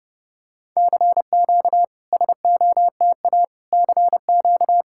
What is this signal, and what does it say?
Morse code, heard as a keyed tone, about 30 wpm, 700 Hz. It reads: CQ SOTA CQ
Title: SOTA QSO Practice 30wpm